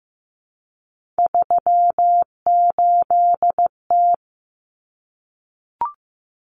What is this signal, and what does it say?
Morse code, heard as a keyed tone, about 15 wpm, 700 Hz. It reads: 38T
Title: Common Cut Numbers - 3-Digit 15wpm